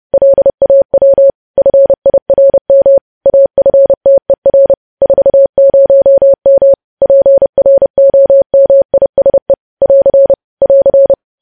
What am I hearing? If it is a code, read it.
Morse code, heard as a keyed tone, about 30 wpm, 550 Hz. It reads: LAW FIRM AFTER 40M PROMISE <AR> <AR>
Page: news headlines in morse code at 30 wpm